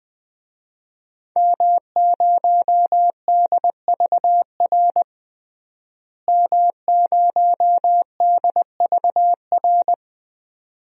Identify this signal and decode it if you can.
Morse code, heard as a keyed tone, about 20 wpm, 700 Hz. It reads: M0DVR M0DVR
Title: DXCC-100 Call Signs - 5 Characters - Repeated 2x - 20wpm